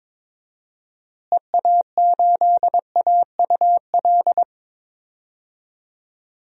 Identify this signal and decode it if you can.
Morse code, heard as a keyed tone, about 22 wpm, 700 Hz. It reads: EA8AUL
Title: DXCC-100 Call Signs - 6 Characters - Repeated 3x - 22wpm